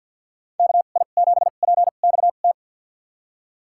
Morse code, heard as a keyed tone, about 52 wpm, spaced as for 36 wpm, 700 Hz. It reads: KI7PXT
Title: USA Call Signs - 2x Character Spacing - 50wpm